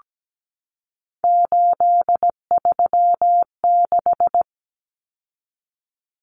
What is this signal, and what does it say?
Morse code, heard as a keyed tone, about 17 wpm, 700 Hz. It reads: 836
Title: Numbers - 3-Digit 17wpm